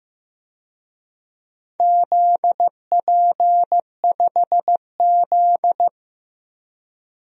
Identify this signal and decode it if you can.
Morse code, heard as a keyed tone, about 15 wpm, 700 Hz. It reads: ZP5Z
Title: DXCC-100 Call Signs - 4 Characters - Repeated 2x - 15wpm